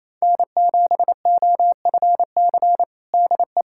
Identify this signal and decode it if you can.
Morse code, heard as a keyed tone, about 28 wpm, 700 Hz. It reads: N7OFC DE
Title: USA Callsign DE Callsign - 5 Characters - 28wpm